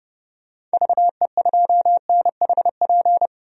Morse code, heard as a keyed tone, about 30 wpm, 700 Hz. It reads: VE2NHP
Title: DXCC-100 Call Signs - 6 Characters - 30wpm